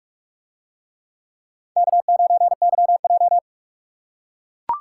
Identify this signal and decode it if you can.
Morse code, heard as a keyed tone, about 41 wpm, 700 Hz. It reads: K9YJ